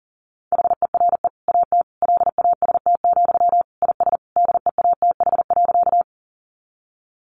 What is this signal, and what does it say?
Morse code, heard as a keyed tone, about 40 wpm, 700 Hz. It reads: HERE AT LAST, IS DEATH.
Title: Sentences from Top 1100 Words 40wpm